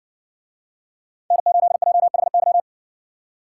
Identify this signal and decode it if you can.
Morse code, heard as a keyed tone, about 56 wpm, 700 Hz. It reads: N8JHY